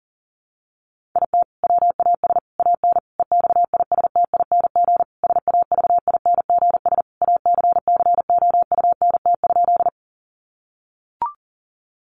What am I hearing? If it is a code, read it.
Morse code, heard as a keyed tone, about 40 wpm, 700 Hz. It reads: IT WAS AN EXISTING SAVINGS ACCOUNT?
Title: Sentences from Top 1500 Words 40wpm